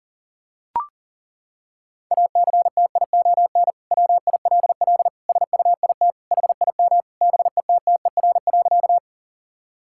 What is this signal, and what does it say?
Morse code, heard as a keyed tone, about 40 wpm, 700 Hz. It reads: ACTION WILL SUIT HIM BETTER.
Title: Sentences from Top 1600 Words 40wpm